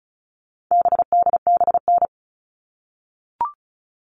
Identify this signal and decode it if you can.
Morse code, heard as a keyed tone, about 34 wpm, 700 Hz. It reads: BDBN
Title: All Cut Numbers - 4-Digit 35wpm